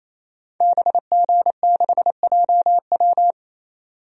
Morse code, heard as a keyed tone, about 28 wpm, 700 Hz. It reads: BG6JW
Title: DXCC-100 Call Signs - 5 Characters - 28wpm